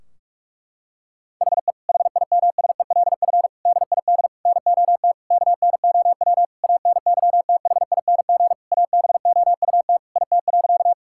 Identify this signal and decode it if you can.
Morse code, heard as a keyed tone, about 46 wpm, 700 Hz. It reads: HE HIMSELF DID NOT KNOW ANYTHING ABOUT IT.
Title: Sentences from Top 1600 Words 45wpm